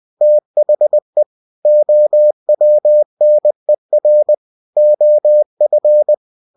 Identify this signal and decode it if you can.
Morse code, heard as a keyed tone, about 20 wpm, 600 Hz. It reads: THE OWNER OF